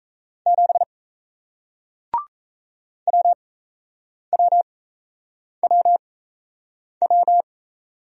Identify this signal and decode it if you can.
Morse code, heard as a keyed tone, about 38 wpm, spaced as for 8 wpm, 700 Hz. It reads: 7 WWWW